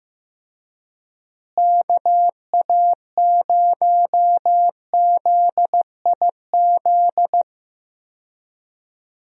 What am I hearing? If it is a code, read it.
Morse code, heard as a keyed tone, about 15 wpm, 700 Hz. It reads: KA0ZIZ
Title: USA Call Signs - 6 Characters - Repeated 3x - 15wpm